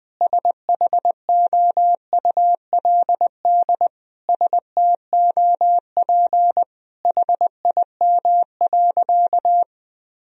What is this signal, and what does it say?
Morse code, heard as a keyed tone, about 20 wpm, 700 Hz. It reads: SHOULD STOP HIM.
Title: Sentences from Top 1000 Words - Review Part 1 - 20wpm